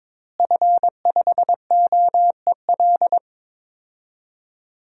Morse code, heard as a keyed tone, about 22 wpm, 700 Hz. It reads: F5OEL
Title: DXCC-100 Call Signs - 5 Characters - Repeated 2x - 22wpm